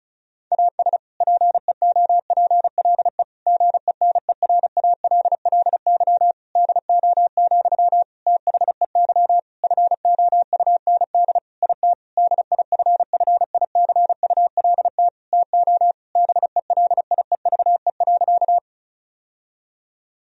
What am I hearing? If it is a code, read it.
Morse code, heard as a keyed tone, about 34 wpm, 700 Hz. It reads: AS PEOPLE GENERALLY DO, THEY FOUND IT DIFFICULT TO BELIEVE.
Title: Sentences from Top 1000 Words - Review Part 4 - 35wpm